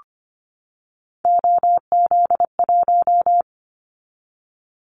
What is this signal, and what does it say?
Morse code, heard as a keyed tone, about 25 wpm, 700 Hz. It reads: OZ1
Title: Callsign Prefixes for Top 100 DXCC - 25wpm